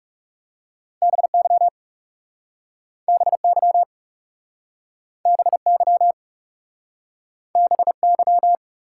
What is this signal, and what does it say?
Morse code, heard as a keyed tone, about 37 wpm, 700 Hz. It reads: BY BY BY BY
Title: Speed-Racing Top 180 QSO Elements 30wpm